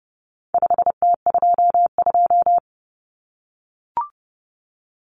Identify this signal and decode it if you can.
Morse code, heard as a keyed tone, about 30 wpm, 700 Hz. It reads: 5T22
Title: Common Cut Numbers - 4-Digit 30wpm